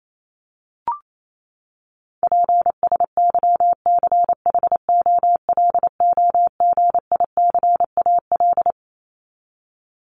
Morse code, heard as a keyed tone, about 28 wpm, 700 Hz. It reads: PSYCHOLOGICAL
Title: Top 1251-1500 Words 28wpm